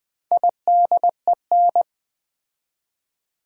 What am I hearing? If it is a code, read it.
Morse code, heard as a keyed tone, about 20 wpm, 700 Hz. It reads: IDEN
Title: Build A Word - 4 Letters At A Time - 8-Letter Words - 20wpm